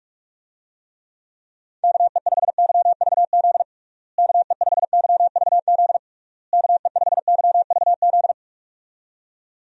Morse code, heard as a keyed tone, about 46 wpm, 700 Hz. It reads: KE5YVZ KE5YVZ KE5YVZ
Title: USA Call Signs - 6 Characters - Repeated 3x - 45wpm